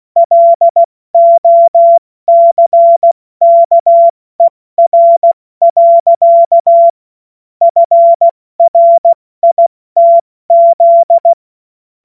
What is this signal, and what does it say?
Morse code, heard as a keyed tone, about 16 wpm, 680 Hz. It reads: LOCKER. FRITZ
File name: Morsekurs / Witze / Witz 07. September 2021, 70 F80